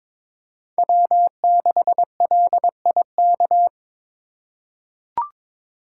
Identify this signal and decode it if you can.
Morse code, heard as a keyed tone, about 22 wpm, 700 Hz. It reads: W6LIK
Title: USA Call Signs - 4x Character Spacing - 22wpm